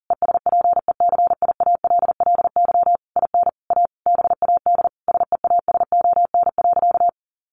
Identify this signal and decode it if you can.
Morse code, heard as a keyed tone, about 40 wpm, 700 Hz. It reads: ESPECIALLY IN A BAD SEASON.
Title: Sentences from Top 1400 Words 40wpm